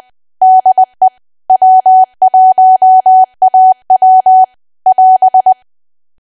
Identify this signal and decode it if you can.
Morse code, heard as a keyed tone, about 20 wpm, 750 Hz. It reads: DE W1AW <AS>
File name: cwa / W1AW-Level3 / 20WPM